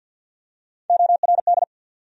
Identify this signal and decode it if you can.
Morse code, heard as a keyed tone, about 53 wpm, 700 Hz. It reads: ORD
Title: USA Call Sign Postfixes - 3 Letters 50wpm